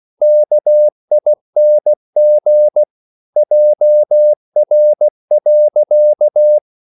Morse code, heard as a keyed tone, about 16 wpm, 600 Hz. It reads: KING JR.
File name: CW / Practice / EnglishQuotations0116